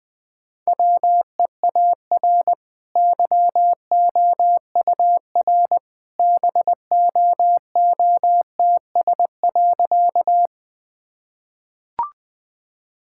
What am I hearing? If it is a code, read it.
Morse code, heard as a keyed tone, about 20 wpm, 700 Hz. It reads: WEAR YOUR BOOTS.
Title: Easy Sentences with Hints - 20wpm